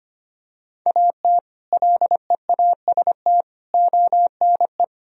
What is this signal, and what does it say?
Morse code, heard as a keyed tone, about 25 wpm, 700 Hz. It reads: AT LEAST ONE
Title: Sentences from Top 1000 Words - Review Part 1 - 25wpm